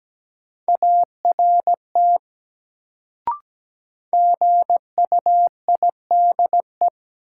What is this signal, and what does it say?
Morse code, heard as a keyed tone, about 17 wpm, 700 Hz. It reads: ART GUIDE
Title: Sentences from Top 600 Words - New Word Drill - 17wpm